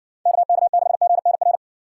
Morse code, heard as a keyed tone, about 59 wpm, 700 Hz. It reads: KK6GNR